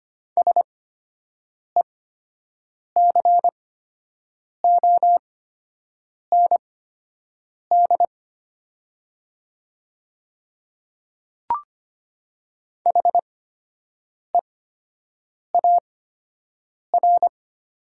Morse code, heard as a keyed tone, about 25 wpm, spaced as for 7 wpm, 700 Hz. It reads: SECOND HEAR
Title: 8x Character Spacing - Top 300 Words - 25wpm